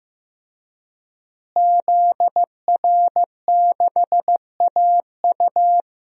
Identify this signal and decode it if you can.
Morse code, heard as a keyed tone, about 15 wpm, 700 Hz. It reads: ZR6AU
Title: DXCC-100 Call Signs - 5 Characters - Repeated 3x - 15wpm